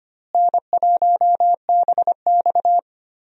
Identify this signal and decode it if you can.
Morse code, heard as a keyed tone, about 25 wpm, 700 Hz. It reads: N1BX
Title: USA Call Signs - 4 Characters - Repeated 3x - 25wpm